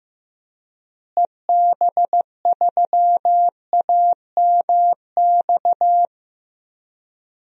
Morse code, heard as a keyed tone, about 15 wpm, 700 Hz. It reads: EB3AMX
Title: DXCC-100 Call Signs - 6 Characters - Repeated 3x - 15wpm